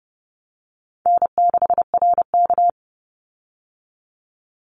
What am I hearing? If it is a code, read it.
Morse code, heard as a keyed tone, about 30 wpm, 700 Hz. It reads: N6RK
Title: Call Signs - 30wpm